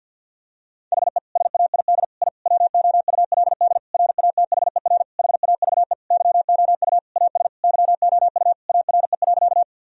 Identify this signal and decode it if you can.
Morse code, heard as a keyed tone, about 50 wpm, 700 Hz. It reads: HE SAID I WOULD RATHER HAVE YOU AS YOU ARE.
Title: Sentences from Top 1000 Words - Review Part 4 - 50wpm